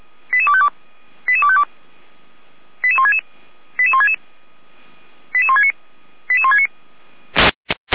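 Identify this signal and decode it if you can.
Five-tone selective call (ZVEI), heard as quick runs of five tones, tones of about 70 ms. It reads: 88252 88252 88166 88166 88169 88169
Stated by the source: Five-tone ZVEI selective call received by SDR, Feuerwehr Orlamünde.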